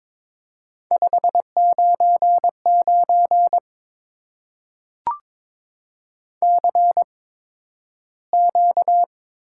Morse code, heard as a keyed tone, about 22 wpm, spaced as for 12 wpm, 700 Hz. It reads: 599 CQ
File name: Top 90 QSO Elements - 8x Character Spacing - 22wpm